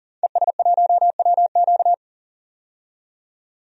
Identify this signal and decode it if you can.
Morse code, heard as a keyed tone, about 40 wpm, 700 Hz. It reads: ES1WQ
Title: DXCC-100 Call Signs - 5 Characters - 40wpm